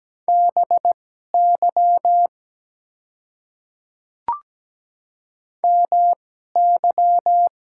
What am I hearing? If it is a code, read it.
Morse code, heard as a keyed tone, about 17 wpm, spaced as for 12 wpm, 690 Hz. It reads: BY MY
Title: 2x Character Spacing - Top 100 Words - 17wpm